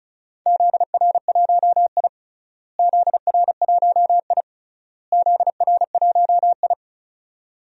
Morse code, heard as a keyed tone, about 34 wpm, 700 Hz. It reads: ZR1I ZR1I ZR1I
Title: DXCC-100 Call Signs - 4 Characters - Repeated 3x - 35wpm